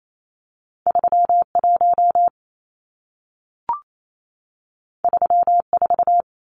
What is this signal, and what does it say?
Morse code, heard as a keyed tone, about 28 wpm, 700 Hz. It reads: 31 34
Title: Numbers - 2-Digit 28wpm